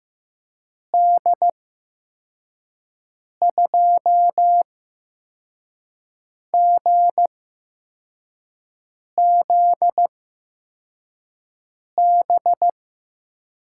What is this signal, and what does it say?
Morse code, heard as a keyed tone, about 15 wpm, spaced as for 4 wpm, 700 Hz. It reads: D2GZB
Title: USA Call Signs - 6 Characters - 8x Character Spacing - 15wpm